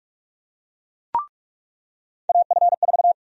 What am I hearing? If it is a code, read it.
Morse code, heard as a keyed tone, about 46 wpm, 700 Hz. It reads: AF4
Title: USA Call Sign Prefixes - 45wpm